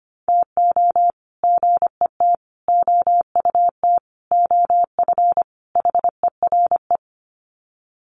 Morse code, heard as a keyed tone, about 25 wpm, 700 Hz. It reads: TO GET OUT OF HERE
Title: Sets of 5 Words Encore 25wpm